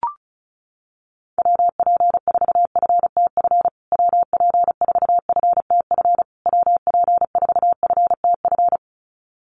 Morse code, heard as a keyed tone, about 34 wpm, 700 Hz. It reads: WP4FTF WP4FTF WP4FTF
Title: USA Call Signs Repeated 3x 35wpm